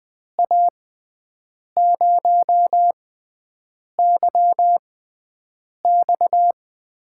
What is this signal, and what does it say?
Morse code, heard as a keyed tone, about 20 wpm, spaced as for 7 wpm, 700 Hz. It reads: A0YX
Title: USA Call Signs - 6 Characters - 6x Character Spacing - 20wpm